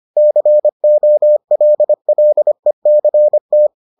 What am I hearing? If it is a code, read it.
Morse code, heard as a keyed tone, about 25 wpm, 600 Hz. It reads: COLLECT